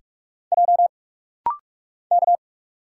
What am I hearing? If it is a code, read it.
Morse code, heard as a keyed tone, about 48 wpm, 700 Hz. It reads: J K
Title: Single Letter-Number - Rapid-Fire 45wpm